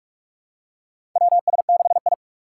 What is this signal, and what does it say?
Morse code, heard as a keyed tone, about 46 wpm, 700 Hz. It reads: WS6I
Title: USA Call Signs - 4 Characters - 3x Character Spacing - 45wpm